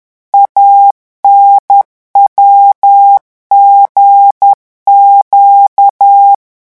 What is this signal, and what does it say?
Morse code, heard as a keyed tone, about 10 wpm, 800 Hz. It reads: ANWGQ
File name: morse mp3 / revision6-vitesse 10 mots